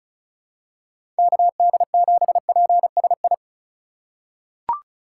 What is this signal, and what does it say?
Morse code, heard as a keyed tone, about 36 wpm, 700 Hz. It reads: KD7PSI
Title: USA Call Signs - 6 Characters - 3x Character Spacing - 35wpm